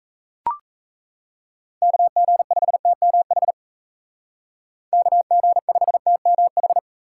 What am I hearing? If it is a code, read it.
Morse code, heard as a keyed tone, about 33 wpm, 700 Hz. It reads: KG5TMH KG5TMH
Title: Speed Racing - USA 2x3 Call Signs 28wpm